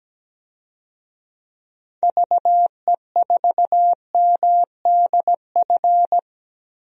Morse code, heard as a keyed tone, about 17 wpm, 700 Hz. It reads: VE4MDF
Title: DXCC-100 Call Signs - 6 Characters - 17wpm